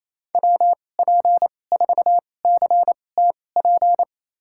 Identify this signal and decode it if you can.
Morse code, heard as a keyed tone, about 28 wpm, spaced as for 20 wpm, 700 Hz. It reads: WP4CTP
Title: USA Call Signs - 6 Characters - 2x Character Spacing - 28wpm